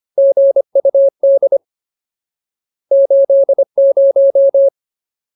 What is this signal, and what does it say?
Morse code, heard as a keyed tone, about 25 wpm, 550 Hz. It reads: GUD 80
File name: Ultimate Word List / 25WPM Ultimate